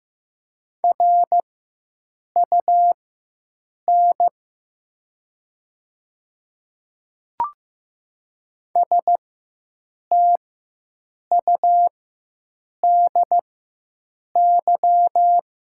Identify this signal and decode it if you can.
Morse code, heard as a keyed tone, about 15 wpm, spaced as for 7 wpm, 700 Hz. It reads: RUN STUDY
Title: 4x Character Spacing - Top 300 Words - 15wpm